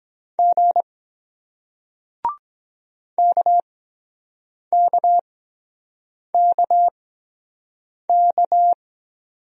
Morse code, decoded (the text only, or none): G KKKK